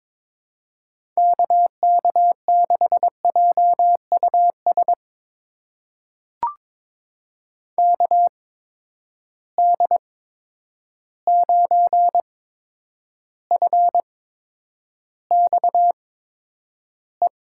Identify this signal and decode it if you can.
Morse code, heard as a keyed tone, about 21 wpm, spaced as for 9 wpm, 700 Hz. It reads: KK6JUS KD9FXE
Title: USA Call Signs - 6 Characters - 8x Character Spacing - 22wpm